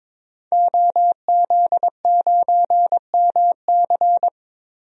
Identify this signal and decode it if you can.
Morse code, heard as a keyed tone, about 22 wpm, 700 Hz. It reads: OZ9MC
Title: DXCC-100 Call Signs - 5 Characters - 22wpm